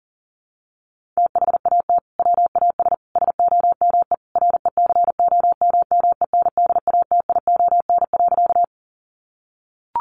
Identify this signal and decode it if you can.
Morse code, heard as a keyed tone, about 40 wpm, 700 Hz. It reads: THAT WAS SOME RECOMMENDATION.
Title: Sentences from Top 1400 Words 40wpm